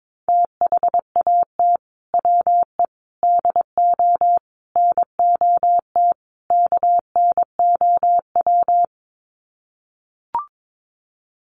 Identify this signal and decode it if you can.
Morse code, heard as a keyed tone, about 22 wpm, 700 Hz. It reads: THAT WE DO NOT KNOW
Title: Sets of 5 Words Encore2 22wpm